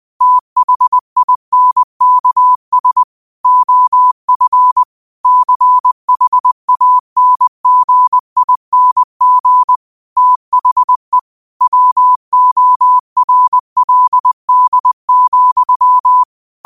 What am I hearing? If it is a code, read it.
Morse code, heard as a keyed tone, about 20 wpm, 1000 Hz. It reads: THINKS OF CHANGING THE WORLD,